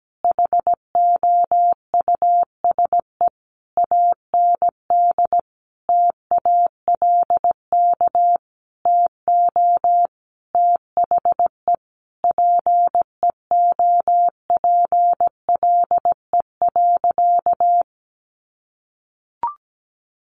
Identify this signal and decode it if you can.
Morse code, heard as a keyed tone, about 17 wpm, 700 Hz. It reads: HOUSE AND TALK TO THE PEOPLE.
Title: Sentences from Top 400 Words 17wpm